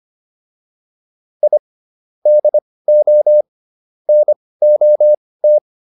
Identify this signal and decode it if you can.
Morse code, heard as a keyed tone, about 25 wpm, spaced as for 18 wpm, 600 Hz. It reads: I DO NOT